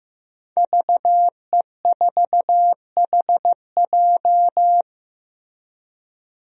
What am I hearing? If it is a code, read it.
Morse code, heard as a keyed tone, about 15 wpm, 700 Hz. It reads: VE4HJ
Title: DXCC-100 Call Signs - 5 Characters - Repeated 2x - 15wpm